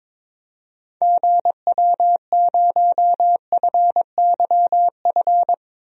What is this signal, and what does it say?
Morse code, heard as a keyed tone, about 22 wpm, 700 Hz. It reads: GW0FYF
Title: DXCC-100 Call Signs - 6 Characters - Repeated 3x - 22wpm